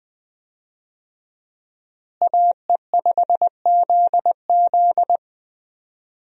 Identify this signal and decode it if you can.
Morse code, heard as a keyed tone, about 20 wpm, 700 Hz. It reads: AE5ZZ
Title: USA Call Signs - 5 Characters - Repeated 2x - 20wpm